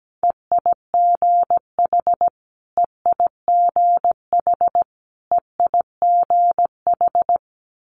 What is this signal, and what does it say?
Morse code, heard as a keyed tone, about 17 wpm, 700 Hz. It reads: EIGH EIGH EIGH
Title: Multi-letter Phonemes Repeated 3x 17wpm